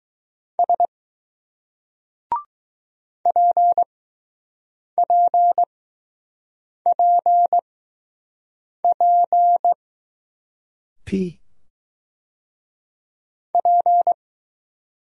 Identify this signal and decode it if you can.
Morse code, heard as a keyed tone, about 20 wpm, spaced as for 6 wpm, 700 Hz. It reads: S PPPP P